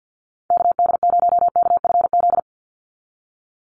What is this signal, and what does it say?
Morse code, heard as a keyed tone, about 50 wpm, 700 Hz. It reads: KD0KFZ
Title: USA Call Signs 50wpm